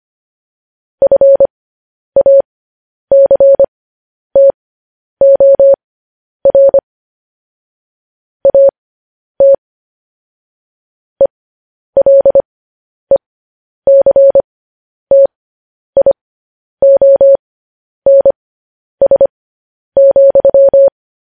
Morse code, heard as a keyed tone, about 25 wpm, spaced as for 10 wpm, 550 Hz. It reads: FACTOR AT ELECTIONS,
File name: podcast / News headlines-1x-25x10